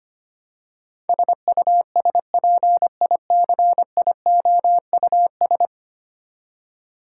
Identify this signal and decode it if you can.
Morse code, heard as a keyed tone, about 25 wpm, 700 Hz. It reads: SUSPICIOUS
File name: Ten Letter Words - 25wpm